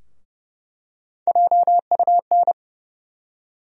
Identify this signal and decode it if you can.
Morse code, heard as a keyed tone, about 30 wpm, 700 Hz. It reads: JUN
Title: USA Call Sign Postfixes - 3 Letters 30wpm